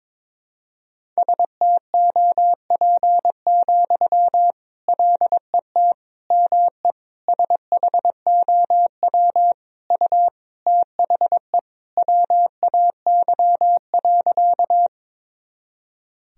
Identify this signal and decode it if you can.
Morse code, heard as a keyed tone, about 22 wpm, 700 Hz. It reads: STOP, LET ME SHOW U THE WAY.
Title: Sentences from Top 300 Words and Abbreviations 22wpm